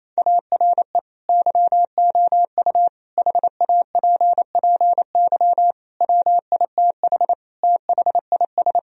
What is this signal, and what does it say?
Morse code, heard as a keyed tone, about 28 wpm, 700 Hz. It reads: ARE YOU HAPPY WITH THIS
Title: Sentences from Top 1000 Words - Review Part 2 - 28wpm